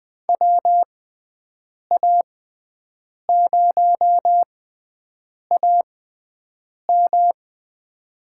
Morse code, heard as a keyed tone, about 21 wpm, spaced as for 7 wpm, 700 Hz. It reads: WA0AM
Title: USA Call Signs - 5 Characters - 6x Character Spacing - 20wpm